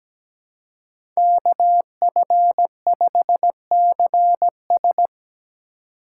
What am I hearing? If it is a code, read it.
Morse code, heard as a keyed tone, about 17 wpm, 700 Hz. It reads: KF5CS
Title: USA Call Signs - 5 Characters - 6x Character Spacing - 17wpm